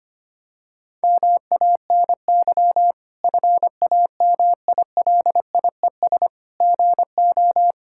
Morse code, heard as a keyed tone, about 25 wpm, 700 Hz. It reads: MANY FAMILIES GO
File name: practice / 25wpm / Difficult Sentences with Hints - 25wpm